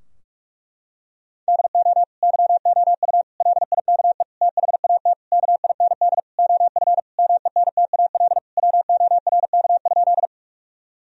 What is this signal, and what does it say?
Morse code, heard as a keyed tone, about 46 wpm, 700 Hz. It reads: DO YOU LIKE THAT KIND OF MENTAL WORK?
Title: Sentences from Top 1000 Words - Review Part 4 - 45wpm